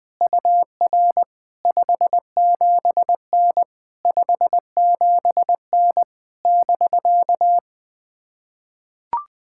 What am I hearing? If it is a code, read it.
Morse code, heard as a keyed tone, about 20 wpm, 660 Hz. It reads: UR 57N 57N <BK>
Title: POTA QSO Practice - 20wpm